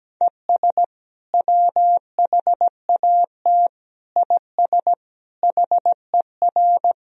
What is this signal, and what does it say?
Morse code, heard as a keyed tone, about 17 wpm, 700 Hz. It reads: ES WHAT IS HER
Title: Sentences from Top 200 Words and Abbreviations 17wpm